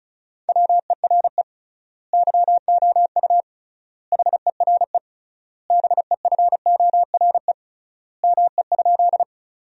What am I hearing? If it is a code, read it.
Morse code, heard as a keyed tone, about 34 wpm, 700 Hz. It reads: WERE YOU HERE BEFORE ME?
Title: 3x Word-Spacing - Sentences from Top 200 Words - 35wpm